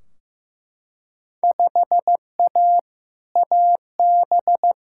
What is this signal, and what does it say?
Morse code, heard as a keyed tone, about 15 wpm, 700 Hz. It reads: 5A AB
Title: Field Day Exchanges Encore - 15wpm